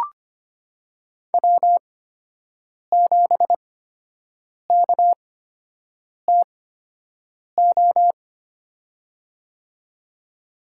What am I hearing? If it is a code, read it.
Morse code, heard as a keyed tone, about 25 wpm, spaced as for 7 wpm, 700 Hz. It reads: W7KTO
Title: USA Call Signs - 8x Character Spacing - 25wpm